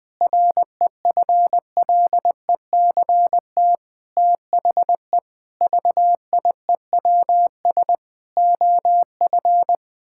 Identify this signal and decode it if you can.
Morse code, heard as a keyed tone, about 20 wpm, 700 Hz. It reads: REFLECT THE VIEWS OF
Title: Sentences with 8-Word Phrases - 20wpm